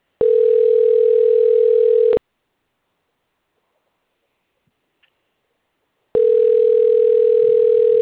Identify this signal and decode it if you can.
Ringback tone (North America), 440 + 480 Hz.
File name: rg-1600-unknown- - - .17656